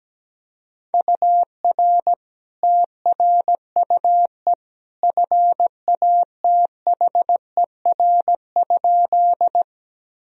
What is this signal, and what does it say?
Morse code, heard as a keyed tone, about 17 wpm, 700 Hz. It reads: UR TRUE FATHER?
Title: Sentences from Top 400 Words and Abbreviations 17wpm